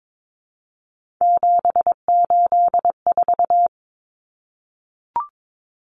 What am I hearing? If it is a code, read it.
Morse code, heard as a keyed tone, about 22 wpm, 700 Hz. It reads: 784
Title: Numbers - 3-Digit 22wpm